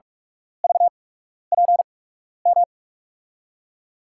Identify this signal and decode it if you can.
Morse code, heard as a keyed tone, about 47 wpm, spaced as for 12 wpm, 700 Hz. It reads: VPM
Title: USA Call Signs - 6 Characters - 8x Character Spacing - 45wpm